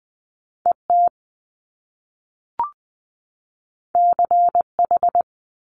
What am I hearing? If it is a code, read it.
Morse code, heard as a keyed tone, about 20 wpm, 700 Hz. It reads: ET CH
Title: Top Two Letter Combinations in Words 20wpm